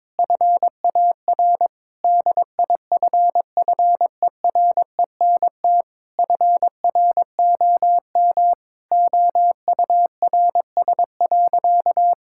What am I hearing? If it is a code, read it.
Morse code, heard as a keyed tone, about 22 wpm, 700 Hz. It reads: FAR DIFFERENT FROM OURS.
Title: Sentences from Top 1000 Words - Review Part 5 - 22wpm